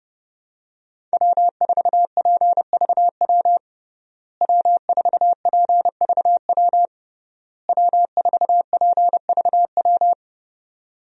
Morse code, heard as a keyed tone, about 30 wpm, 700 Hz. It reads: W4PVW W4PVW W4PVW
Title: USA Call Signs - 5 Characters - Repeated 3x - 30wpm